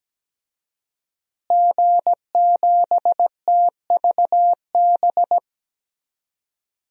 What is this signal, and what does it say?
Morse code, heard as a keyed tone, about 17 wpm, 700 Hz. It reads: G7TVB
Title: DXCC-100 Call Signs - 5 Characters - Repeated 3x - 17wpm